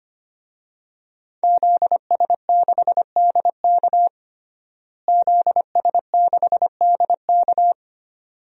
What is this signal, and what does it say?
Morse code, heard as a keyed tone, about 25 wpm, 700 Hz. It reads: ZS6DK ZS6DK
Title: DXCC-100 Call Signs - 5 Characters - Repeated 2x - 25wpm